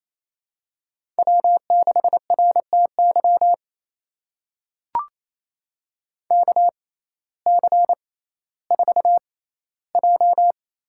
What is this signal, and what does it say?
Morse code, heard as a keyed tone, about 28 wpm, spaced as for 15 wpm, 700 Hz. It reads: W6RTY KC4J
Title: USA Call Signs - 6x Character Spacing - 28wpm